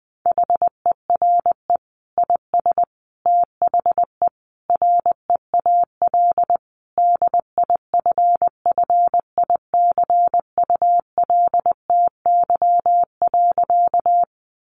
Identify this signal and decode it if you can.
Morse code, heard as a keyed tone, about 20 wpm, 700 Hz. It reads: HERE IS THE REAL DIFFICULTY.
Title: Sentences from Top 1500 Words 20wpm